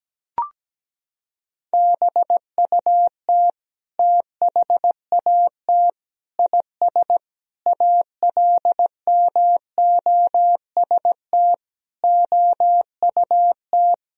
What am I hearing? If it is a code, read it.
Morse code, heard as a keyed tone, about 17 wpm, 700 Hz. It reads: BUT THAT IS ALMOST OUT
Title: Sentences from Top 1000 Words - Review Part 4 - 17wpm